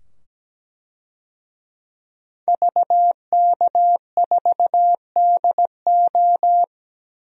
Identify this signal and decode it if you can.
Morse code, heard as a keyed tone, about 17 wpm, 700 Hz. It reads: VK4DO
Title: DXCC-100 Call Signs - 5 Characters - 17wpm